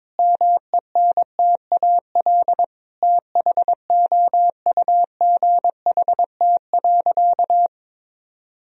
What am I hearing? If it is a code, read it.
Morse code, heard as a keyed tone, about 22 wpm, 700 Hz. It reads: MENTAL THOUGHT.
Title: Sentences from Top 1000 Words - Review Part 4 - 22wpm